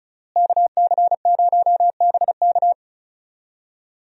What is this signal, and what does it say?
Morse code, heard as a keyed tone, about 36 wpm, 700 Hz. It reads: KC0BK
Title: USA Call Signs - 5 Characters - Repeated 3x - 35wpm